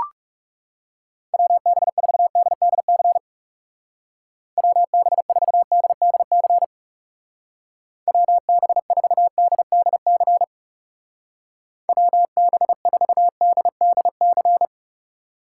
Morse code, heard as a keyed tone, about 37 wpm, 700 Hz. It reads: WB4DDC WB4DDC WB4DDC WB4DDC
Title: Speed Racing - USA 2x3 Call Signs 30wpm